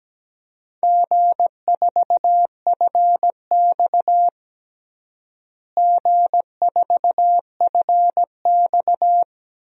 Morse code, heard as a keyed tone, about 17 wpm, 700 Hz. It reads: G4FX G4FX
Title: DXCC-100 Call Signs - 4 Characters - Repeated 3x - 17wpm